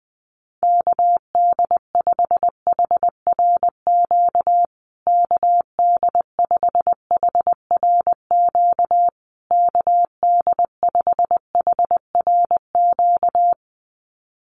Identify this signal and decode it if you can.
Morse code, heard as a keyed tone, about 20 wpm, 700 Hz. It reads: KD5HRQ KD5HRQ KD5HRQ
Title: USA Call Signs Repeated 3x 20wpm